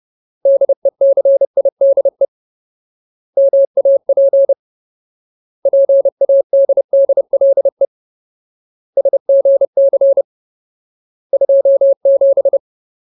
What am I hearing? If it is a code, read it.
Morse code, heard as a keyed tone, about 30 wpm, 550 Hz. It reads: DECIDE MAP PADDLE SGC 27
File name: Ultimate Word List / 30WPM Ultimate